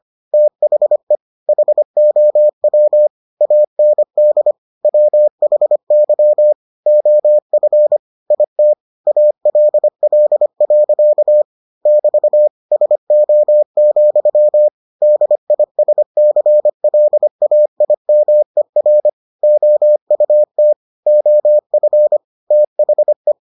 Morse code, heard as a keyed tone, about 25 wpm, 600 Hz. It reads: THE HOW AND WHY OF IT ALL. <BT> SO, DISCLAIMER OUT OF THE